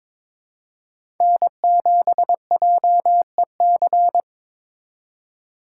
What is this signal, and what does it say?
Morse code, heard as a keyed tone, about 22 wpm, 700 Hz. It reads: N7JEC
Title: USA Call Signs - 5 Characters - 4x Character Spacing - 22wpm